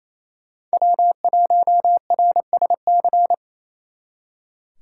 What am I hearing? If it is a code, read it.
Morse code, heard as a keyed tone, about 35 wpm, 700 Hz. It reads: W1RSC